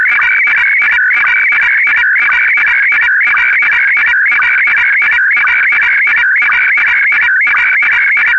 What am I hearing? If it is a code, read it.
SSTV (slow-scan television), Scottie DX.
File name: scottie dx